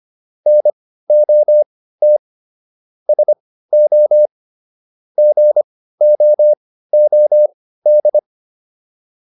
Morse code, heard as a keyed tone, about 25 wpm, spaced as for 15 wpm, 600 Hz. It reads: NOT SO GOOD